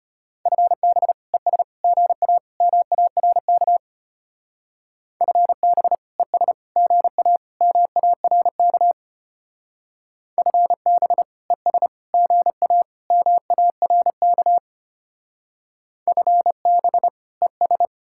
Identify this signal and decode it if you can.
Morse code, heard as a keyed tone, about 33 wpm, 700 Hz. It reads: FB ES GA MARK FB ES GA MARK FB ES GA MARK FB ES